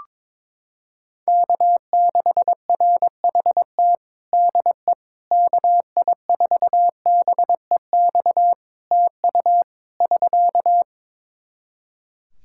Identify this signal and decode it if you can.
Morse code, heard as a keyed tone, about 22 wpm, 700 Hz. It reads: K6RHT DE KI4BEX TU <SK>
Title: K1USN SST QSO Practice 22wpm